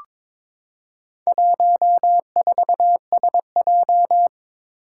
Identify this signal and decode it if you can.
Morse code, heard as a keyed tone, about 22 wpm, 700 Hz. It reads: 14SJ
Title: Four Challenging Letter-Number Combinations 22wpm